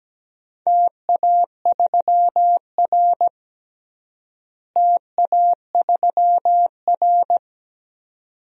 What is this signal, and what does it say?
Morse code, heard as a keyed tone, about 17 wpm, 700 Hz. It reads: TA3R TA3R
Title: DXCC-100 Call Signs - 4 Characters - Repeated 2x - 17wpm